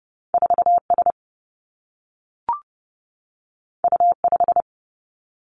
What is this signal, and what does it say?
Morse code, heard as a keyed tone, about 30 wpm, 700 Hz. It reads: 4 S U 5
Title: Two Challenging Letter-Number Combinations 30wpm